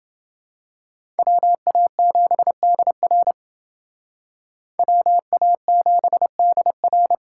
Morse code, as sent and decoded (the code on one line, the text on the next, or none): .-- .- --... -.. .-. / .-- .- --... -.. .-.
WA7DR WA7DR